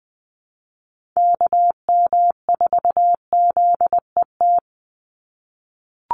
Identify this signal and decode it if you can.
Morse code, heard as a keyed tone, about 20 wpm, 700 Hz. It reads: KM4ZET
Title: USA Call Signs 20wpm